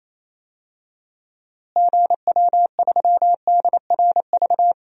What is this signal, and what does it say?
Morse code, heard as a keyed tone, about 28 wpm, 700 Hz. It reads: GW3DRV
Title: DXCC-100 Call Signs - 6 Characters - Repeated 3x - 28wpm